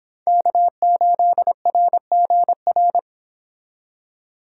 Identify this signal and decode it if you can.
Morse code, heard as a keyed tone, about 31 wpm, 700 Hz. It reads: K8RGR